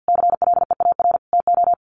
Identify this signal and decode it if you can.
Morse code, heard as a keyed tone, about 50 wpm, 700 Hz. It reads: CLEAR TO
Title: Sentences from Top 1300 Words 50wpm